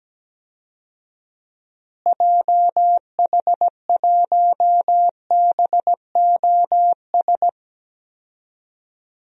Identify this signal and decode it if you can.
Morse code, heard as a keyed tone, about 17 wpm, 700 Hz. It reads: JH1BOS
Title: DXCC-100 Call Signs - 6 Characters - Repeated 2x - 17wpm